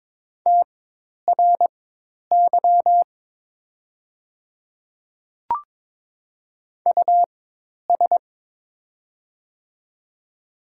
Morse code, heard as a keyed tone, about 22 wpm, spaced as for 10 wpm, 690 Hz. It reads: TRY US
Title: 4x Character Spacing - Top 200 Words - 22wpm